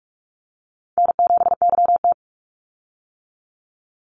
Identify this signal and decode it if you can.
Morse code, heard as a keyed tone, about 44 wpm, 700 Hz. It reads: N7YT
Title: Easy Call Signs - 45wpm